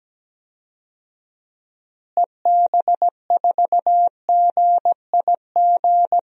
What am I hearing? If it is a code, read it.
Morse code, heard as a keyed tone, about 17 wpm, 700 Hz. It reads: EB4GIG
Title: DXCC-100 Call Signs - 6 Characters - Repeated 2x - 17wpm